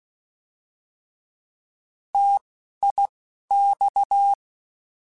Morse code, heard as a keyed tone, about 16 wpm, 800 Hz. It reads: T I X
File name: Contrasts / Contrasts-mp3s / 3LtrContrasts @10